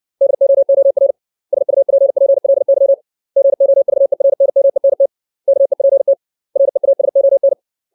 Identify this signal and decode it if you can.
Morse code, heard as a keyed tone, about 60 wpm, 550 Hz. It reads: DOOR SUPPLY GOVERNMENT KEPT REASON
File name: 632 words / 632 Words 60WPM